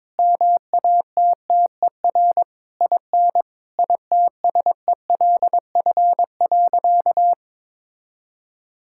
Morse code, heard as a keyed tone, about 22 wpm, 700 Hz. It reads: MATTER IN ITSELF.
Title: Sentences from Top 1000 Words - Review Part 5 - 22wpm